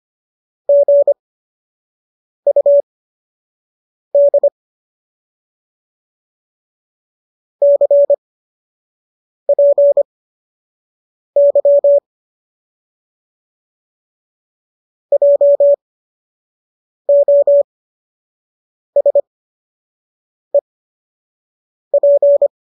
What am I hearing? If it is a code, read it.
Morse code, heard as a keyed tone, about 25 wpm, spaced as for 6 wpm, 570 Hz. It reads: GUD CPY JOSEP